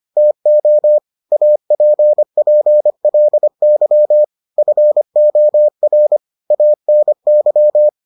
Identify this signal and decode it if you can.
Morse code, heard as a keyed tone, about 25 wpm, 600 Hz. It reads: TO APPLY FOR ANY